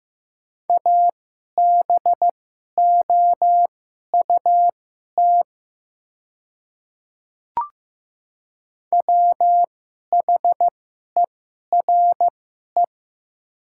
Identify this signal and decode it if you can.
Morse code, heard as a keyed tone, about 15 wpm, spaced as for 11 wpm, 700 Hz. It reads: ABOUT WHERE
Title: 2x Character Spacing - Top 300 Words - 15wpm